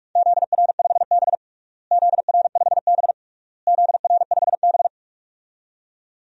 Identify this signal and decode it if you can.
Morse code, heard as a keyed tone, about 42 wpm, 690 Hz. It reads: ZR5B ZR5B ZR5B
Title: DXCC-100 Call Signs - 4 Characters - Repeated 3x - 45wpm